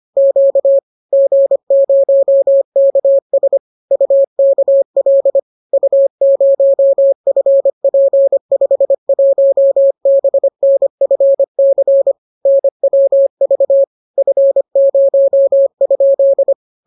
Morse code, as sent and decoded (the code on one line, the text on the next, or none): --.- / --. ----- -.- ... / ..- -.- .-.. / ..- ----- ..-. .--. ..... .---- -... -. ..-. -.-. / -. .-- ...- / ..-. ----- ..--..
Q G0KS UKL U0FP51BNFC NWV F0?